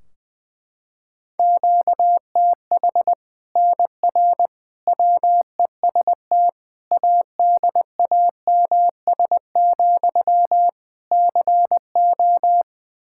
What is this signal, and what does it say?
Morse code, heard as a keyed tone, about 20 wpm, 700 Hz. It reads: QTH NR WEST ADAMS, CO
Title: Practice QSO - Exchange 1 - Beginning - 20wpm